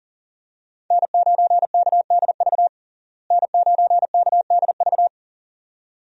Morse code, heard as a keyed tone, about 40 wpm, 700 Hz. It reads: N9KDV N9KDV
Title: USA Call Signs - 5 Characters - Repeated 2x - 40wpm